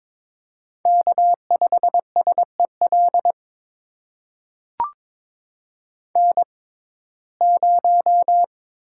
Morse code, heard as a keyed tone, about 22 wpm, spaced as for 16 wpm, 700 Hz. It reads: K5SEL N0
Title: USA Call Signs - 6x Character Spacing - 22wpm